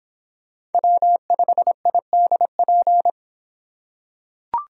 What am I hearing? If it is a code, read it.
Morse code, heard as a keyed tone, about 24 wpm, 700 Hz. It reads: W5IDP